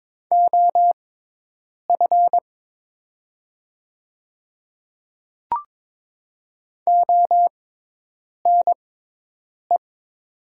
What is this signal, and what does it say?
Morse code, heard as a keyed tone, about 22 wpm, spaced as for 8 wpm, 700 Hz. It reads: OF ONE
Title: 6x Character Spacing - Top 300 Words - 22wpm